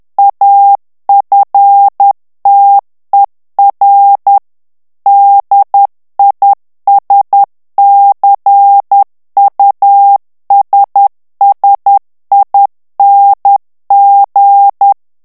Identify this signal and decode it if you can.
Morse code, heard as a keyed tone, about 10 wpm, 800 Hz. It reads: AFTER DISCUSSING
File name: ANGLAIS texte clair 3 vitesse 10 mots minute